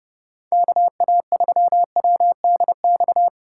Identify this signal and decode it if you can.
Morse code, heard as a keyed tone, about 30 wpm, 700 Hz. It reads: KA3WDX